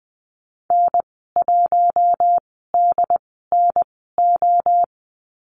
Morse code, heard as a keyed tone, about 20 wpm, spaced as for 15 wpm, 700 Hz. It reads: N1DNO
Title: 088 Lesson 15g - Number 5 - Call Signs 20wpm